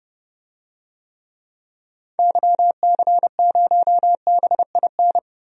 Morse code, heard as a keyed tone, about 30 wpm, 700 Hz. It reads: YC0BIN
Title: DXCC-100 Call Signs - 6 Characters - Repeated 3x - 30wpm